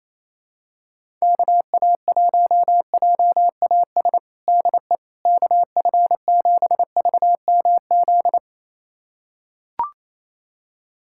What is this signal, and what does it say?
Morse code, heard as a keyed tone, about 27 wpm, 700 Hz. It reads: KA1JAS DE KF7VMZ
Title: USA Callsign DE Callsign - 6 Characters - 28wpm